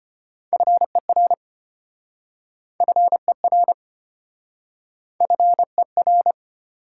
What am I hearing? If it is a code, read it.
Morse code, heard as a keyed tone, about 33 wpm, 690 Hz. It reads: FER FER FER